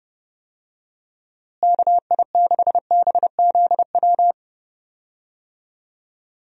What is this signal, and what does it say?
Morse code, heard as a keyed tone, about 30 wpm, 700 Hz. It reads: KI6BZW
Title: USA Call Signs - 6 Characters - Repeated 3x - 30wpm